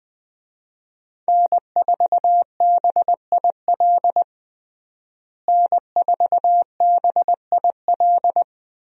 Morse code, heard as a keyed tone, about 20 wpm, 700 Hz. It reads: N4BIL N4BIL
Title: USA Call Signs - 5 Characters - Repeated 2x - 20wpm